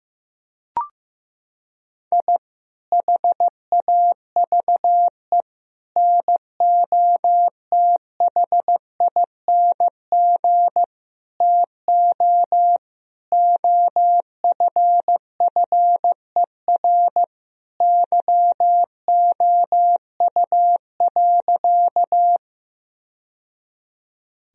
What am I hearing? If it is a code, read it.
Morse code, heard as a keyed tone, about 14 wpm, 700 Hz. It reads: I HAVE NOTHING TO OFFER YOU.
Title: Sentences from Top 1000 Words - Review Part 4 - 15wpm